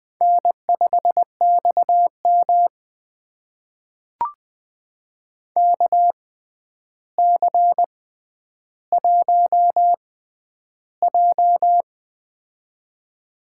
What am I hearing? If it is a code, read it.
Morse code, heard as a keyed tone, about 20 wpm, spaced as for 10 wpm, 700 Hz. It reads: N5XM KC1J
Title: USA Call Signs - 4 Characters - 6x Character Spacing - 20wpm